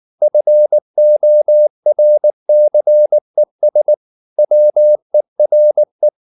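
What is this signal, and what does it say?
Morse code, heard as a keyed tone, about 19 wpm, 600 Hz. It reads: FORCES WERE
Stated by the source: Morse code at 19 wpm from CW med Gnister, reading Ken Follett's novel Never in English.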